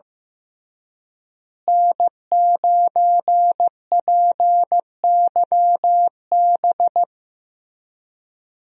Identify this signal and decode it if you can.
Morse code, heard as a keyed tone, about 15 wpm, 700 Hz. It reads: N9PYB
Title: USA Call Signs - 5 Characters - Repeated 3x - 15wpm